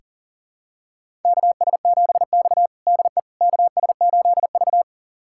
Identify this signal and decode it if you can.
Morse code, heard as a keyed tone, about 40 wpm, 700 Hz. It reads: KS7X DE KS8V
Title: USA Callsign DE Callsign - 4 Characters - 40wpm